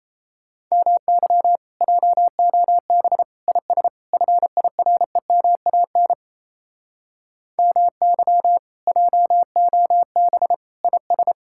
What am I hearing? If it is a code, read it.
Morse code, heard as a keyed tone, about 37 wpm, 700 Hz. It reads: MY JOB IS FIREMAN MY JOB IS